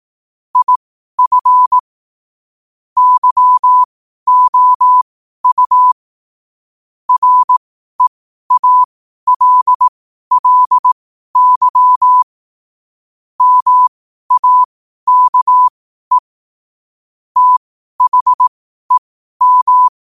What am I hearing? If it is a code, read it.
Morse code, heard as a keyed tone, about 18 wpm, spaced as for 12 wpm, 1000 Hz. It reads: IF YOU REALLY MAKE THEM
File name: qotdfiles / 12wpm